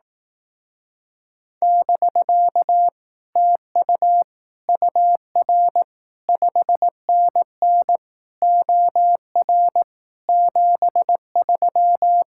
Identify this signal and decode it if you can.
Morse code, heard as a keyed tone, about 18 wpm, 700 Hz. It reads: <BK> TU UR 5NN OR 73
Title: Speed-Racing Common QSO Phrases Encore - 15wpm